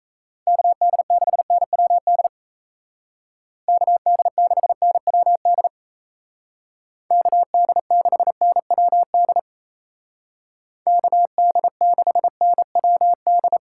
Morse code, heard as a keyed tone, about 34 wpm, 700 Hz. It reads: KD6NWD KD6NWD KD6NWD KD6NWD
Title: Speed Racing - USA 2x3 Call Signs 28wpm